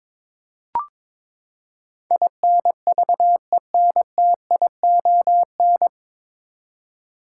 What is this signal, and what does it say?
Morse code, heard as a keyed tone, about 23 wpm, 700 Hz. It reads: INVENTION
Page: Morse Code Ninja practice 22wpm